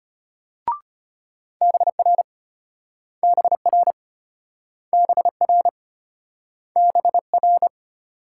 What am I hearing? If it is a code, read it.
Morse code, heard as a keyed tone, about 34 wpm, 700 Hz. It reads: BR BR BR BR